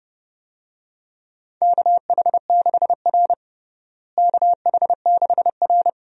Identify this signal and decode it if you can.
Morse code, heard as a keyed tone, about 30 wpm, 700 Hz. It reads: KH6R KH6R
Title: USA Call Signs - 4 Characters - Repeated 3x - 30wpm